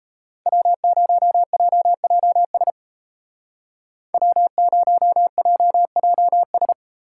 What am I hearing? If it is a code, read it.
Morse code, heard as a keyed tone, about 33 wpm, 700 Hz. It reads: W0JJS W0JJS
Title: Speed Racing - USA 1x3 Call Signs 28wpm